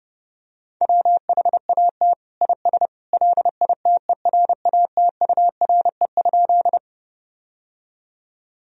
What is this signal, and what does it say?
Morse code, heard as a keyed tone, about 30 wpm, 700 Hz. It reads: WHAT IS LITERATURE?
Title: Sentences from Top 1000 Words - Review Part 3 - 30wpm